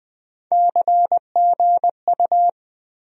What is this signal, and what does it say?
Morse code, heard as a keyed tone, about 20 wpm, 700 Hz. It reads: CGU
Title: USA Call Sign Postfixes - 3 Letters 20wpm